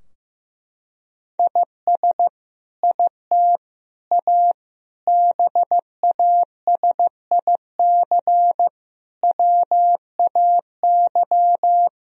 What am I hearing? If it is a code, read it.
Morse code, heard as a keyed tone, about 15 wpm, 700 Hz. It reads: IS IT A BASIC WAY
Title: Sentences from Top 1000 Words - Review Part 3 - 15wpm